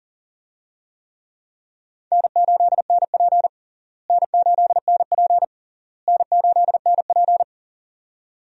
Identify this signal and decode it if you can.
Morse code, heard as a keyed tone, about 37 wpm, 700 Hz. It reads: N8NP N8NP N8NP
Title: USA Call Signs - 4 Characters - Repeated 3x - 40wpm